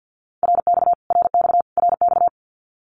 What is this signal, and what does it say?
Morse code, heard as a keyed tone, about 50 wpm, 700 Hz. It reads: RX RX RX
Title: Abbreviations Repeated 3x - 50wpm